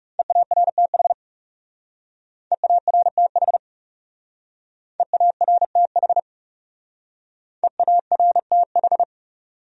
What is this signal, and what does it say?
Morse code, heard as a keyed tone, about 37 wpm, 700 Hz. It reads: EARTH EARTH EARTH EARTH
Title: Speed-Racing Top 300 Words 30wpm